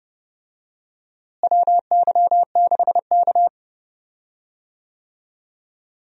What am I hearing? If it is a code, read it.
Morse code, heard as a keyed tone, about 30 wpm, 700 Hz. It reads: WY6K
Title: USA Call Signs - 4 Characters - Repeated 3x - 30wpm